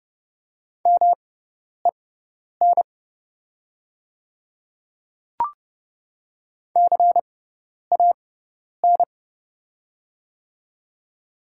Morse code, heard as a keyed tone, about 30 wpm, spaced as for 10 wpm, 700 Hz. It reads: MEN CAN
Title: 6x Character Spacing - Top 200 Words - 30wpm